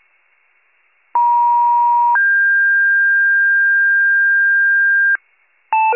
Two-tone page, heard as a one-second tone, then a three-second tone, about 950 then 1600 Hz, with noise. It is heard from 1.1 to 5.2 s.
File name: pager / 83 RESCUE DUTY / wcalert